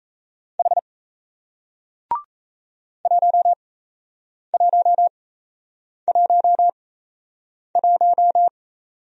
Morse code, heard as a keyed tone, about 38 wpm, spaced as for 8 wpm, 700 Hz. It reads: H 1111